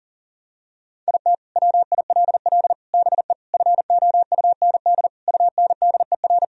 Morse code, heard as a keyed tone, about 40 wpm, 700 Hz. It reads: IT WILL BE FOUND UNDER
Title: Sentences from Top 1000 Words - Review Part 1 - 40wpm